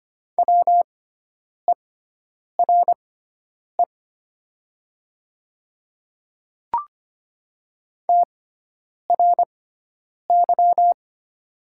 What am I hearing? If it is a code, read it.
Morse code, heard as a keyed tone, about 25 wpm, spaced as for 9 wpm, 690 Hz. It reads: WERE TRY
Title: 6x Character Spacing - Top 200 Words - 25wpm